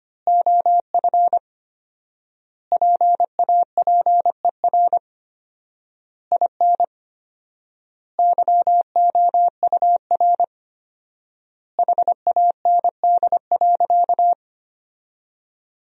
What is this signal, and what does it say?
Morse code, heard as a keyed tone, about 25 wpm, 700 Hz. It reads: OF PAPER IN YOUR HAND.
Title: 4x Word-Spacing - Sentences from Top 400 Words - 25wpm